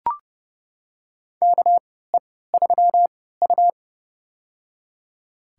Morse code, heard as a keyed tone, about 30 wpm, spaced as for 17 wpm, 700 Hz. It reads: KE3U
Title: USA Call Signs - 3x Character Spacing - 30wpm